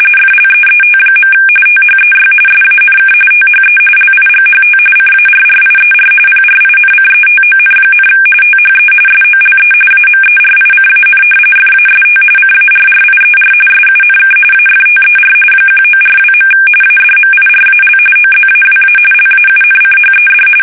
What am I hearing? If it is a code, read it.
RTTY, about 45.45 baud. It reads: FLEET. THEY SANK THE LIGHT CARRIER PRINCETON, THE SOLE AMERICAN FAST CARRIER SUNK AFTER 1942. ADMIRAL OZAWA LAUNCHED
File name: fsk850 / battleshipnj